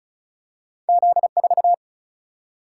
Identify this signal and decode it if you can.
Morse code, heard as a keyed tone, about 36 wpm, 700 Hz. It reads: Z 4
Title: Single-Letter Single-Number 35wpm